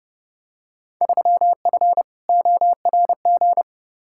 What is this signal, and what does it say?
Morse code, heard as a keyed tone, about 32 wpm, 700 Hz. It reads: 3F ORG
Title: Field Day Exchanges - 30wpm